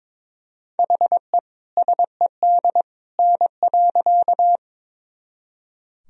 Morse code, heard as a keyed tone, about 22 wpm, 700 Hz. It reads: HE SED N.
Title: Sentences from Top 100 Words and Abbreviations 22wpm